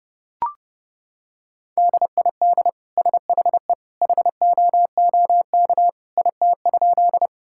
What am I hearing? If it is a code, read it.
Morse code, heard as a keyed tone, about 30 wpm, 700 Hz. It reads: DID SHE HOOK IT?
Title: Sentences from Top 1600 Words 30wpm